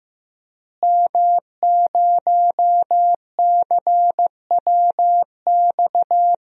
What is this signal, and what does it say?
Morse code, heard as a keyed tone, about 16 wpm, 700 Hz. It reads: M0CWX
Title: DXCC-100 Call Signs - 5 Characters - Repeated 3x - 15wpm